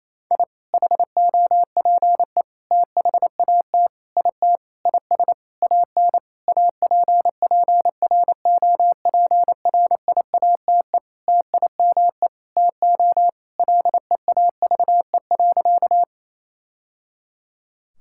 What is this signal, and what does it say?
Morse code, heard as a keyed tone, about 28 wpm, 700 Hz. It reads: I HOPE THAT IT IS AN APPROPRIATE TIME TO LEAVE.
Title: Sentences from Top 1000 Words - Review Part 5 - 28wpm